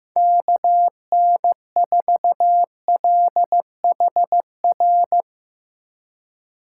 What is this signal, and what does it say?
Morse code, heard as a keyed tone, about 15 wpm, 700 Hz. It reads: KN4LHR
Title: Speed Racing - USA Call Signs 15wpm